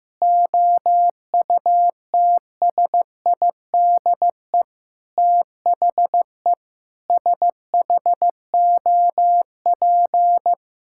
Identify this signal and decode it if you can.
Morse code, heard as a keyed tone, about 15 wpm, 700 Hz. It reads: OUTSIDE THE SHOP
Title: Sentences from Top 1600 Words 15wpm